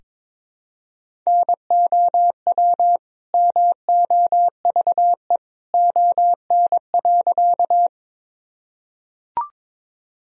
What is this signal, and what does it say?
Morse code, heard as a keyed tone, about 21 wpm, 700 Hz. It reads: NOW MOVE ON.
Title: Sentences from Top 1000 Words - Review Part 1 - 22wpm